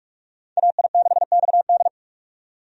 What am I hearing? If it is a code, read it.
Morse code, heard as a keyed tone, about 46 wpm, 700 Hz. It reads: AI6XD
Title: USA Call Signs - 5 Characters - 45wpm